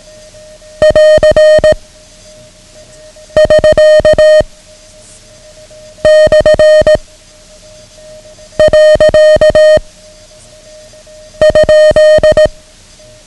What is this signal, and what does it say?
Morse code, heard as a keyed tone, about 18 wpm, 620 Hz. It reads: <AR> <SK> / . ?
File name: ham radio / CodeQuick / Code Quick 09